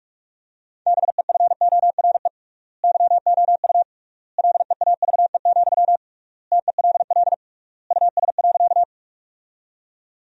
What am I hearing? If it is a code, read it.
Morse code, heard as a keyed tone, about 43 wpm, 700 Hz. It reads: BEFORE YOU LEAVE, TELL US.
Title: 3x Word-Spacing - Sentences from Top 400 Words - 45wpm